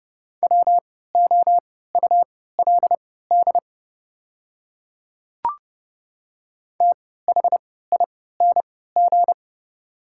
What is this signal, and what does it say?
Morse code, heard as a keyed tone, about 30 wpm, spaced as for 17 wpm, 700 Hz. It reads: WOULD THING
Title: 3x Character Spacing - Top 100 Words - 30wpm